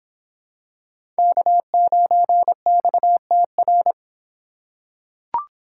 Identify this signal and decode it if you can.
Morse code, heard as a keyed tone, about 26 wpm, 700 Hz. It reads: K9XTR